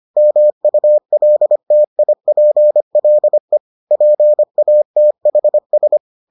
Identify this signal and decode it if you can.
Morse code, heard as a keyed tone, about 25 wpm, 600 Hz. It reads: MULTIPLE PATHS